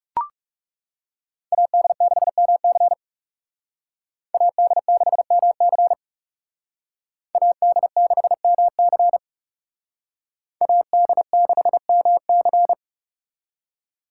Morse code, heard as a keyed tone, about 35 wpm, 700 Hz. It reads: AD6MC AD6MC AD6MC AD6MC
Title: Speed Racing - USA Call Signs 30wpm